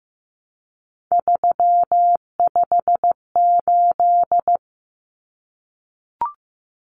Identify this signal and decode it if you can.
Morse code, heard as a keyed tone, about 15 wpm, 700 Hz. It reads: 358
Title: Common Cut Numbers - 3-Digit 15wpm